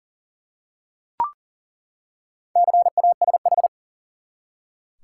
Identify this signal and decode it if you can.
Morse code, heard as a keyed tone, about 39 wpm, 700 Hz. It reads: CASH
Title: Four Letter Words - 40wpm